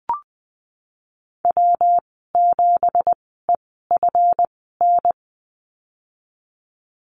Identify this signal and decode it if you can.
Morse code, heard as a keyed tone, about 20 wpm, spaced as for 15 wpm, 700 Hz. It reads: W7EFN
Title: 200 Lesson 32g - Letter k - Call Signs 20wpm